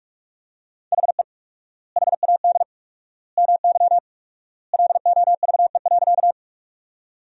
Morse code, heard as a keyed tone, about 46 wpm, 700 Hz. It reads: HE HAD MY LOVE.
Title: 4x Word-Spacing - Sentences from Top 400 Words - 45wpm